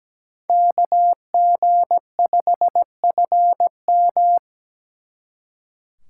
Morse code, heard as a keyed tone, about 20 wpm, 700 Hz. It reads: KG5FM
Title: Speed Racing - USA 2x2 Call Signs 17wpm